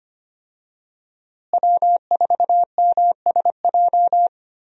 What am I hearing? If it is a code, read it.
Morse code, heard as a keyed tone, about 25 wpm, 690 Hz. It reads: W4MSJ
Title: USA Call Signs - 5 Characters - Repeated 2x - 25wpm